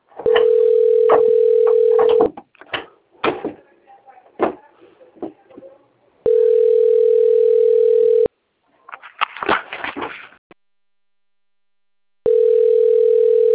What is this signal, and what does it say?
Ringback tone (North America), 440 + 480 Hz.